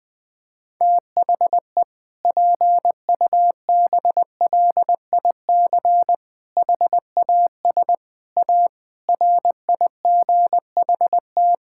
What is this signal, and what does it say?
Morse code, heard as a keyed tone, about 20 wpm, 700 Hz. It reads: THE PUBLIC HAS A RIGHT
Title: Sentences with 7-Word Phrases - Spoken Beginning - 20wpm